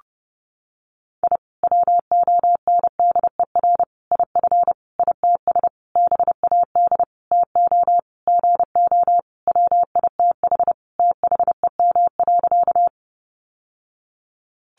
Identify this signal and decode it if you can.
Morse code, heard as a keyed tone, about 30 wpm, 700 Hz. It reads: I WONDER IF ITS BAD TO GO WITH THEM.
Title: Sentences from Top 700 Words 30wpm